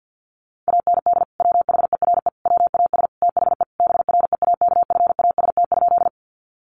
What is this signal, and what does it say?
Morse code, heard as a keyed tone, about 50 wpm, 700 Hz. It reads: AND WHERE WAS THE BREAKFAST?
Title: Sentences from Top 1500 Words 50wpm